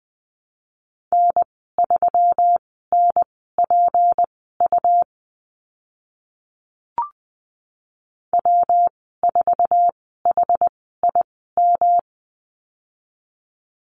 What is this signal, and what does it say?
Morse code, heard as a keyed tone, about 20 wpm, spaced as for 14 wpm, 700 Hz. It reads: N3NPU W4HIM
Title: 154 Lesson 25g - Letter p - Call Signs 20wpm